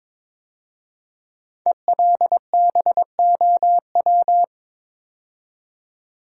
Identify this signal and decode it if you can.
Morse code, heard as a keyed tone, about 22 wpm, 700 Hz. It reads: ELBOW
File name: 5 Letter Words - First 3 Letters Given - 22wpm